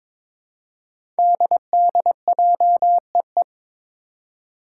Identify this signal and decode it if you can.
Morse code, heard as a keyed tone, about 22 wpm, 700 Hz. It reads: DDJEE
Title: Five Challenging Letter-Number Combinations 22wpm